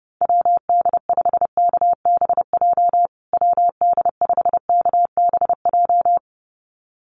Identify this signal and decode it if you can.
Morse code, heard as a keyed tone, about 30 wpm, 700 Hz. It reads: WD5KBJ WD5KBJ
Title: USA Call Signs Repeated 2x 30wpm